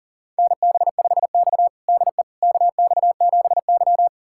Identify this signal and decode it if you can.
Morse code, heard as a keyed tone, about 40 wpm, 700 Hz. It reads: NB5X DE KX7Y
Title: USA Callsign DE Callsign - 4 Characters - 40wpm